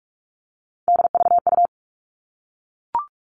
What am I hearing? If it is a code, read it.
Morse code, heard as a keyed tone, about 44 wpm, 700 Hz. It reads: DVU
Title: All Cut Numbers - 3-Digit 45wpm